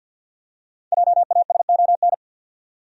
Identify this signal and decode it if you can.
Morse code, heard as a keyed tone, about 48 wpm, 700 Hz. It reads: JASON
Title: USA Common Names - 50wpm